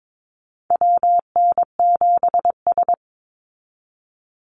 Morse code, heard as a keyed tone, about 22 wpm, 700 Hz. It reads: WN7S
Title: Call Signs Repeated 2x - 22wpm